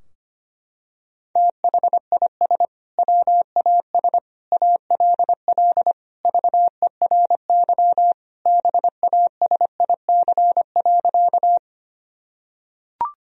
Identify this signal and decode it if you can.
Morse code, heard as a keyed tone, about 25 wpm, 700 Hz. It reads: THIS WAS ALL VERY BASIC.
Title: Sentences from Top 1000 Words - Review Part 3 - 25wpm